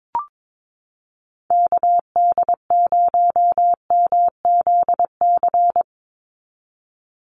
Morse code, heard as a keyed tone, about 22 wpm, 700 Hz. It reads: KD0MZC
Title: USA Call Signs 22wpm